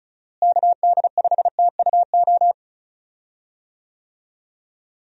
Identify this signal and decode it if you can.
Morse code, heard as a keyed tone, about 36 wpm, 700 Hz. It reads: KD5TUO
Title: USA Call Signs - 6 Characters - Repeated 2x - 35wpm